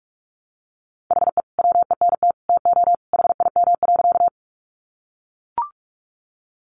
Morse code, heard as a keyed tone, about 44 wpm, 700 Hz. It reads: HE WENT TO HIM.
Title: Sentences from Top 200 Words 45wpm